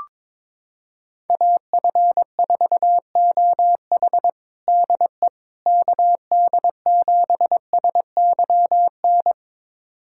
Morse code, heard as a keyed tone, about 22 wpm, 700 Hz. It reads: AF4OH DE KD7SYN
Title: USA Callsign DE Callsign 22wpm